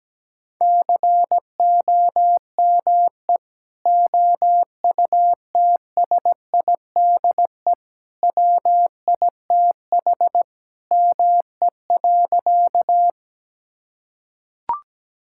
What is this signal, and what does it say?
Morse code, heard as a keyed tone, about 17 wpm, 700 Hz. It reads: COME OUTSIDE WITH ME.
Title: Sentences from Top 1000 Words - Review Part 6 - 17wpm